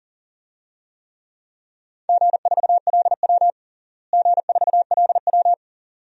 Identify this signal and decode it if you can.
Morse code, heard as a keyed tone, about 37 wpm, 700 Hz. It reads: G4LW G4LW
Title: DXCC-100 Call Signs - 4 Characters - Repeated 2x - 40wpm